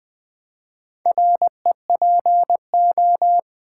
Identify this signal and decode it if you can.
Morse code, heard as a keyed tone, about 20 wpm, 700 Hz. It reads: REPO
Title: Build A Word - 4 Letters At A Time - 10-Letter Words - 20wpm